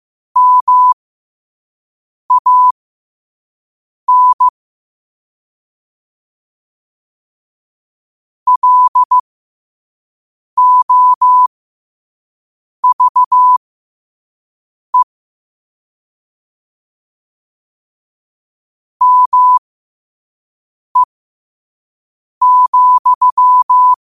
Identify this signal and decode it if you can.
Morse code, heard as a keyed tone, about 15 wpm, spaced as for 5 wpm, 1000 Hz. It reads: MAN LOVE ME,
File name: qotdfiles / 5wpm